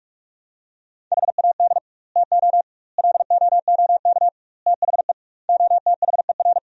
Morse code, heard as a keyed tone, about 46 wpm, 700 Hz. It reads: HAD TO LOOK THE OTHER
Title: 2x Word-Spacing - Sentences from Top 100 Words - 45wpm